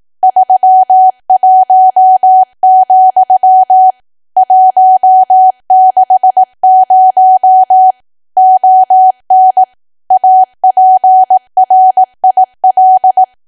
Morse code, heard as a keyed tone, about 18 wpm, 750 Hz. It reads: 31, 160 ON APRIL
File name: cwa / W1AW-Level2 / 18WPM